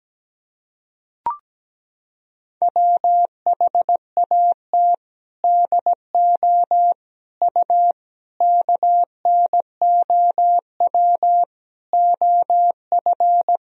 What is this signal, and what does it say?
Morse code, heard as a keyed tone, about 17 wpm, 700 Hz. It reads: WHAT DO U KNOW OF
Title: Sentences from Top 100 Words and Abbreviations 17wpm